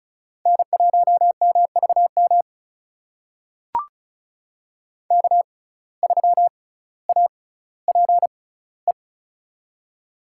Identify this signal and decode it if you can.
Morse code, heard as a keyed tone, about 36 wpm, spaced as for 18 wpm, 700 Hz. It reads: N1MVM K3APE
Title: USA Call Signs - 5 Characters - 6x Character Spacing - 35wpm